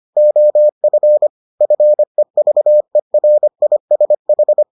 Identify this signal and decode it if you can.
Morse code, heard as a keyed tone, about 25 wpm, 600 Hz. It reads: OF FEVERISH